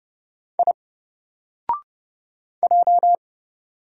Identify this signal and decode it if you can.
Morse code, heard as a keyed tone, about 30 wpm, 700 Hz. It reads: I J
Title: Single Letter-Number 30wpm